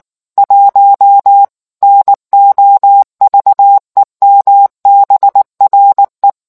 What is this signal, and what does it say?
Morse code, heard as a keyed tone, about 19 wpm, 800 Hz. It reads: 1 NOVEMBRE